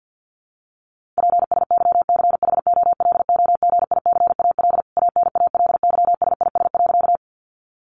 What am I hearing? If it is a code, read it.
Morse code, heard as a keyed tone, about 50 wpm, 700 Hz. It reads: PSYCHOLOGICAL ANALYSIS.
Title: Sentences from Top 1400 Words 50wpm